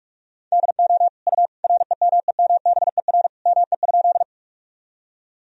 Morse code, heard as a keyed tone, about 46 wpm, 700 Hz. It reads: DO U REMEMBER ME?
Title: Sentences from Top 400 Words and Abbreviations 45wpm